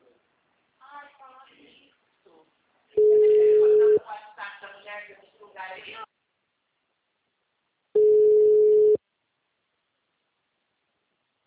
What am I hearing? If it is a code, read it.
Ringback tone (Europe, 425 Hz), with speech.